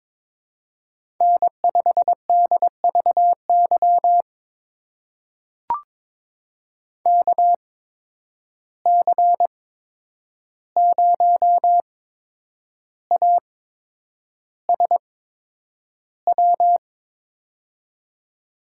Morse code, heard as a keyed tone, about 22 wpm, spaced as for 9 wpm, 700 Hz. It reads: N5DVY KC0ASW
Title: USA Call Signs - 8x Character Spacing - 22wpm